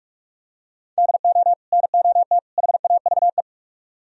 Morse code, heard as a keyed tone, about 47 wpm, 700 Hz. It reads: DO NOT HAVE